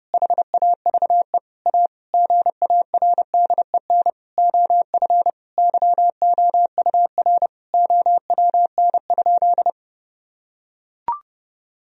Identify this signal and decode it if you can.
Morse code, heard as a keyed tone, about 30 wpm, 700 Hz. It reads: HAVE A GARDEN OF YOUR OWN?
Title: Sentences from Top 1000 Words - Review Part 4 - 30wpm